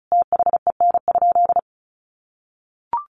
Morse code, heard as a keyed tone, about 34 wpm, 700 Hz. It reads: THEN?
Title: Sentences from Top 1100 Words 35wpm